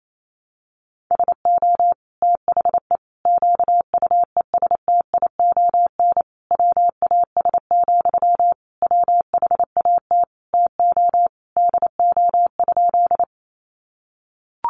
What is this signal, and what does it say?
Morse code, heard as a keyed tone, about 28 wpm, 700 Hz. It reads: SO THE QUESTION WAS, WHAT TO DO?
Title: Sentences from Top 400 Words 28wpm